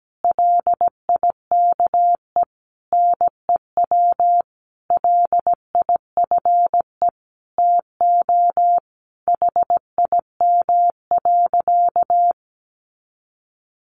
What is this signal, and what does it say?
Morse code, heard as a keyed tone, about 17 wpm, 700 Hz. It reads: LIKE NEW LIFE TO HIM.
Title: Sentences from Top 300 Words 17wpm